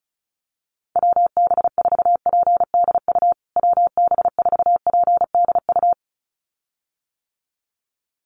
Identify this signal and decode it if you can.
Morse code, heard as a keyed tone, about 34 wpm, 700 Hz. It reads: WB4PDU WB4PDU
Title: USA Call Signs Repeated 2x 35wpm